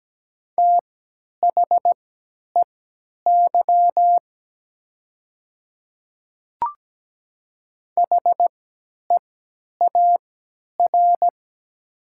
Morse code, heard as a keyed tone, about 17 wpm, spaced as for 10 wpm, 700 Hz. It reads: THEY HEAR
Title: 3x Character Spacing - Top 300 Words - 17wpm